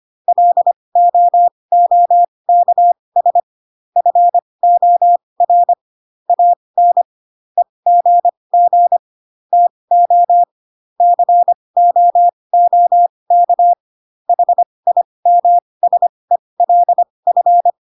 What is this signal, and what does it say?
Morse code, heard as a keyed tone, about 25 wpm, spaced as for 20 wpm, 700 Hz. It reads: LOOKS FOR AN EGG TO COOK HIMSELF
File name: Pinocchio / Pinocchio ch5 15 25 ik1hge